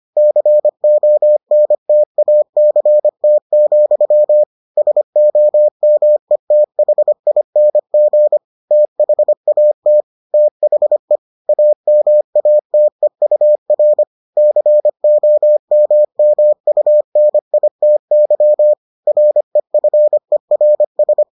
Morse code, heard as a keyed tone, about 25 wpm, 600 Hz. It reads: CONTACT, SOMETHING THAT THE AMATEUR COMMUNITY REFERS